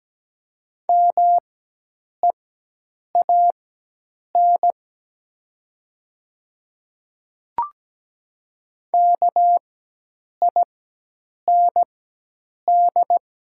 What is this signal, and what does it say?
Morse code, heard as a keyed tone, about 17 wpm, spaced as for 8 wpm, 700 Hz. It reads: MEAN KIND
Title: 4x Character Spacing - Top 300 Words - 17wpm